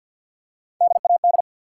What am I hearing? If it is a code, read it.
Morse code, heard as a keyed tone, about 52 wpm, 700 Hz. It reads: DAD
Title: Three Letter Words - 50wpm